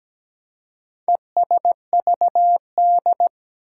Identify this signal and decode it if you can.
Morse code, heard as a keyed tone, about 17 wpm, 700 Hz. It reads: ESVD
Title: Four Challenging Letter-Number Combinations 17wpm